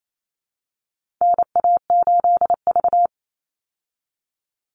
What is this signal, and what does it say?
Morse code, heard as a keyed tone, about 28 wpm, 700 Hz. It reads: NA8V
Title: Call Signs Repeated 3x - 28wpm